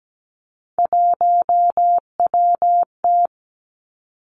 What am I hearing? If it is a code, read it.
Morse code, heard as a keyed tone, about 17 wpm, 700 Hz. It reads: 1WT
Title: Three Challenging Letter-Number Combinations 17wpm